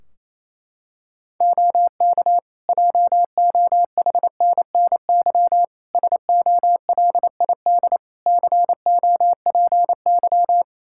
Morse code, heard as a keyed tone, about 28 wpm, 700 Hz. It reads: OK JOHNNY SOLID COPY
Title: Practice QSO - Exchange 3 - More Info - 28wpm